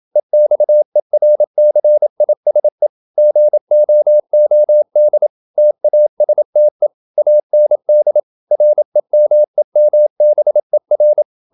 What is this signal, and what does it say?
Morse code, heard as a keyed tone, about 27 wpm, 600 Hz. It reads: EXERCISE GOOD TASTE AND REMEMBER
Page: Never 26wpm | CW med Gnister